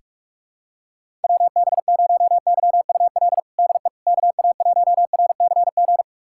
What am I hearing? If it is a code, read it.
Morse code, heard as a keyed tone, about 41 wpm, 700 Hz. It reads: WB0YUL DE KA1RCG
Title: USA Callsign DE Callsign 45wpm